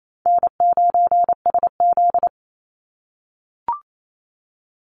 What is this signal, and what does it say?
Morse code, heard as a keyed tone, about 28 wpm, 700 Hz. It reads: N9SZ
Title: Easy Call Signs - 28wpm